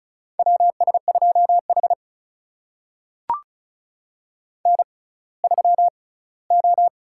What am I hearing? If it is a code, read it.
Morse code, heard as a keyed tone, about 36 wpm, spaced as for 20 wpm, 700 Hz. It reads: WS2H N3O
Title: USA Call Signs - 6x Character Spacing - 35wpm